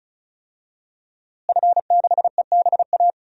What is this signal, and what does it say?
Morse code, heard as a keyed tone, about 34 wpm, 700 Hz. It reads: F6EBA
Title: DXCC-100 Call Signs - 5 Characters - Repeated 3x - 35wpm